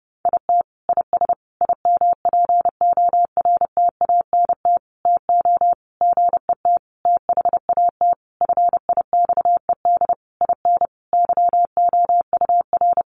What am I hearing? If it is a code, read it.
Morse code, heard as a keyed tone, about 28 wpm, 700 Hz. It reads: IT IS IMPORTANT TO GET THAT FIXED IN YOUR
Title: Sentences from Top 1100 Words 30wpm